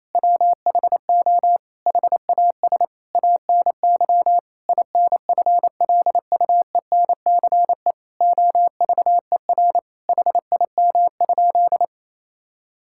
Morse code, heard as a keyed tone, about 28 wpm, 700 Hz. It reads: WHO HAS ANY INFLUENCE OVER HIM?
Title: Sentences from Top 1600 Words 28wpm